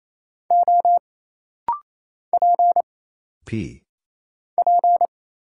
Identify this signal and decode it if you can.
Morse code, heard as a keyed tone, about 27 wpm, 700 Hz. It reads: O P P
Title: Single Letter-Number Alt - Rapid-Fire 28wpm